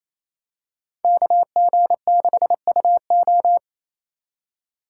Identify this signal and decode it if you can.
Morse code, heard as a keyed tone, about 28 wpm, 700 Hz. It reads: KG6UO
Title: USA Call Signs - 2x Character Spacing - 28wpm